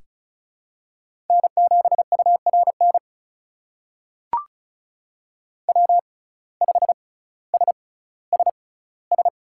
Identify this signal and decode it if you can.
Morse code, heard as a keyed tone, about 34 wpm, spaced as for 18 wpm, 700 Hz. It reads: N7URN W5SSS
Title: USA Call Signs - 5 Characters - 6x Character Spacing - 35wpm